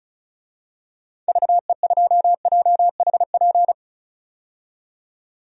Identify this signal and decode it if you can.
Morse code, heard as a keyed tone, about 36 wpm, 700 Hz. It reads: VE2JHP
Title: DXCC-100 Call Signs - 6 Characters - 35wpm